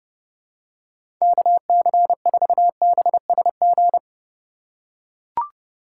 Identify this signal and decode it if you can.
Morse code, heard as a keyed tone, about 30 wpm, 700 Hz. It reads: KC4BSG
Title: USA Call Signs - 6 Characters - 3x Character Spacing - 30wpm